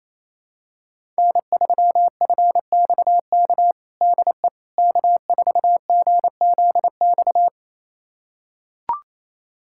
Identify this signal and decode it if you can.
Morse code, heard as a keyed tone, about 27 wpm, 700 Hz. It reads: N3FXK DE K4GZX
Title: USA Callsign DE Callsign - 5 Characters - 28wpm